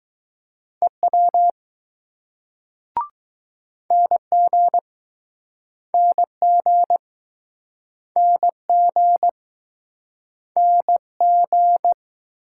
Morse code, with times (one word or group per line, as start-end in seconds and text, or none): 0.8-1.5 EW
3.9-4.8 NG
5.9-7.0 NG
8.2-9.3 NG
10.6-11.9 NG